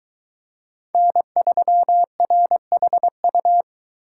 Morse code, heard as a keyed tone, about 23 wpm, 700 Hz. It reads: N3RHU